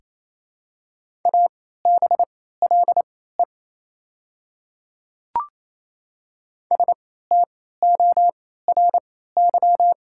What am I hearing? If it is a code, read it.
Morse code, heard as a keyed tone, about 28 wpm, spaced as for 16 wpm, 700 Hz. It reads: ABLE STORY
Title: 3x Character Spacing - Top 500 Words - 28wpm